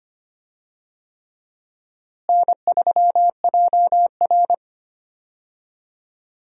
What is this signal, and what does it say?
Morse code, heard as a keyed tone, about 25 wpm, 700 Hz. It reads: N3JR
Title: USA Call Signs - 4 Characters - Repeated 3x - 25wpm